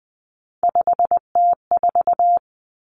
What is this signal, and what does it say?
Morse code, heard as a keyed tone, about 20 wpm, 700 Hz. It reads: 5T4
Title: Common Cut Numbers - 3-Digit 20wpm